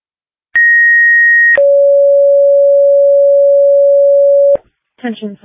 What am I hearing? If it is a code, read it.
Two-tone page, heard as a one-second tone, then a three-second tone, about 1900 then 560 Hz. It is heard from 0.5 to 4.6 s.